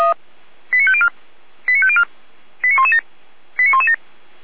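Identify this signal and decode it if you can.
Five-tone selective call (ZVEI), heard as quick runs of five tones, tones of about 70 ms. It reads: 89504 89504 89117 89117
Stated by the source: ZVEI five-tone selective call received on SDR.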